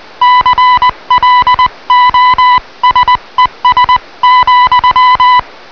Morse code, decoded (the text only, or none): CLOSES,